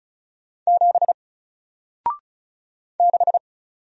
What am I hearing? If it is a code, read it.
Morse code, heard as a keyed tone, about 33 wpm, 700 Hz. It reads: 7 6
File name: Single-Letter-Number-35wpm